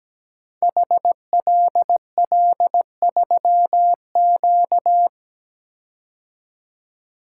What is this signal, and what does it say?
Morse code, heard as a keyed tone, about 17 wpm, 700 Hz. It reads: HLL3Q
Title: Code Groups - 5 Letter:Numbers - 17wpm